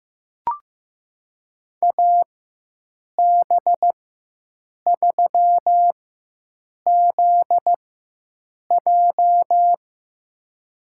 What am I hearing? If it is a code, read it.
Morse code, heard as a keyed tone, about 16 wpm, spaced as for 7 wpm, 700 Hz. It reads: AB3ZJ
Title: USA Call Signs - 5 Characters - 4x Character Spacing - 15wpm